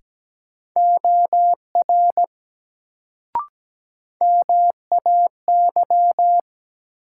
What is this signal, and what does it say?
Morse code, heard as a keyed tone, about 17 wpm, 700 Hz. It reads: OR MAY
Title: Sentences from Top 100 Words - New Word Drill - 17wpm